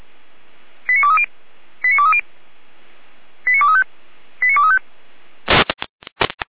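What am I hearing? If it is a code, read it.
Five-tone selective call (ZVEI), heard as quick runs of five tones, tones of about 70 ms. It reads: 89230 89230 89235 89235
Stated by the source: ZVEI five-tone selective call received with an SDR, hosted by Feuerwehr Orlamünde (fire brigade).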